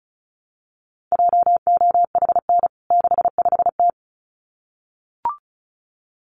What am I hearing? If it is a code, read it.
Morse code, heard as a keyed tone, about 34 wpm, 700 Hz. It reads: JOHN 65T
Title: CWT Exchange 35wpm